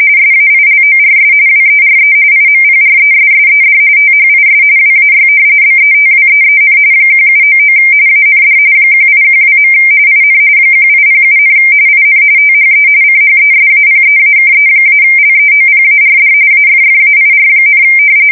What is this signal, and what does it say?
RTTY, about 45.45 baud. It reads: SAILORS WERE IMPRESSED THIS WAY BETWEEN 1799 AND 1812. IN 1807, IN THE CHESAPEAKE-LEOPARD AFFAIR,
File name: fsk170 / usnavy1